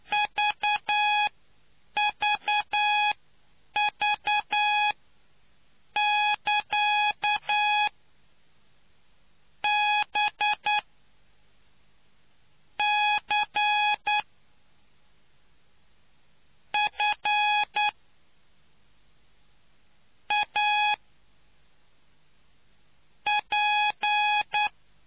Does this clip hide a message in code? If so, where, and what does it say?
Yes: Morse code from 0.1 to 24.7 s, heard as a keyed tone, about 10 wpm, spaced as for 4 wpm, 840 Hz. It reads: VVV<KA>BCFAP